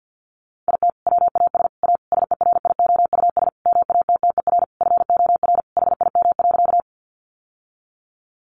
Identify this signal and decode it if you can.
Morse code, heard as a keyed tone, about 50 wpm, 700 Hz. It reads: IT WAS A SERIOUS MATTER FOR HIM.
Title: Sentences from Top 1100 Words 50wpm